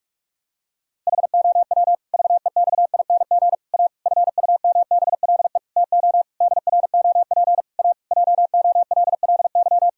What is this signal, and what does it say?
Morse code, heard as a keyed tone, about 41 wpm, 700 Hz. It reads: HOW VEXING A FUMBLE TO DROP A JOLLY
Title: Pangram Challenge v2 - 45wpm